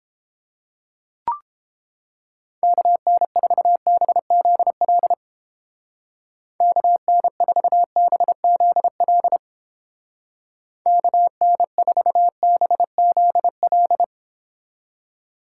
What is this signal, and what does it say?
Morse code, heard as a keyed tone, about 31 wpm, 700 Hz. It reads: KN4BZL KN4BZL KN4BZL